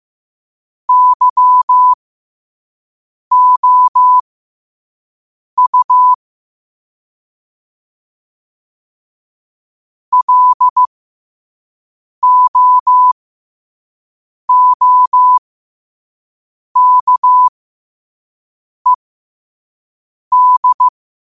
Morse code, heard as a keyed tone, about 15 wpm, spaced as for 5 wpm, 1000 Hz. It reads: YOU LOOKED
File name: qotdfiles / 5wpm